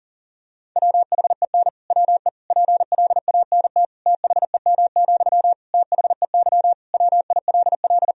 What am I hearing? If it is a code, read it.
Morse code, heard as a keyed tone, about 40 wpm, 700 Hz. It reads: WHEN WE PLANT THEM, THEY WILL
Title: Sentences from Top 1000 Words - Review Part 1 - 40wpm